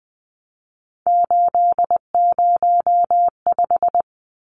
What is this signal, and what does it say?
Morse code, heard as a keyed tone, about 20 wpm, 700 Hz. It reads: 805
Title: Numbers - 3-Digit 20wpm